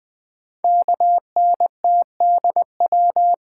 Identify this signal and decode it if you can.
Morse code, heard as a keyed tone, about 20 wpm, 700 Hz. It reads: KNTDW
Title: Code Groups - 5 Letters - 20wpm